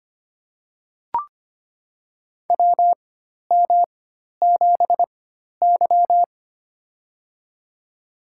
Morse code, heard as a keyed tone, about 25 wpm, spaced as for 12 wpm, 690 Hz. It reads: WM7Y
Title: USA Call Signs - 4x Character Spacing - 25wpm